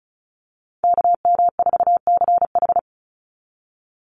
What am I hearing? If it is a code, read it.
Morse code, heard as a keyed tone, about 34 wpm, 700 Hz. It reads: KM4CH
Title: Call Signs Repeated 2x - 35wpm